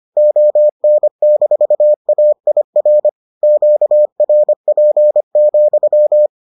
Morse code, heard as a keyed tone, about 25 wpm, 600 Hz. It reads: ON-AIR QRP,